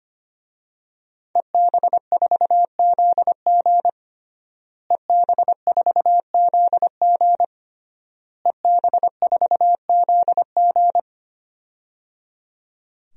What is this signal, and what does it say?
Morse code, heard as a keyed tone, about 25 wpm, 700 Hz. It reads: EB4ZG EB4ZG EB4ZG
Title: DXCC-100 Call Signs - 5 Characters - Repeated 3x - 25wpm